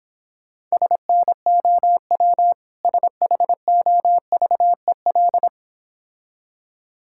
Morse code, heard as a keyed tone, about 26 wpm, 700 Hz. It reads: SNOW SHOVEL